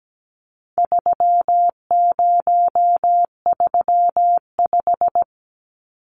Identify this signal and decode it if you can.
Morse code, heard as a keyed tone, about 17 wpm, 700 Hz. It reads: 3035
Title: Numbers - 4-Digit 17wpm